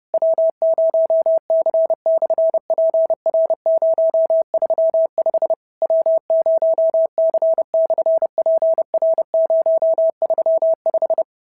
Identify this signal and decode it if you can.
Morse code, heard as a keyed tone, about 30 wpm, 650 Hz. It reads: W0C/PR035 W0C/PR035
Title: SOTA QSO Practice 30wpm